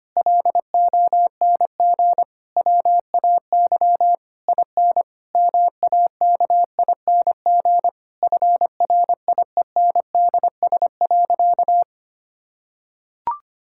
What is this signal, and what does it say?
Morse code, heard as a keyed tone, about 25 wpm, 700 Hz. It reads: LONG WAY IN MAKING FRIENDS.
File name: practice / 25wpm / Difficult Sentences with Hints - 25wpm